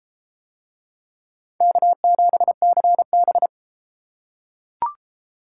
Morse code, heard as a keyed tone, about 33 wpm, 700 Hz. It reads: K7CB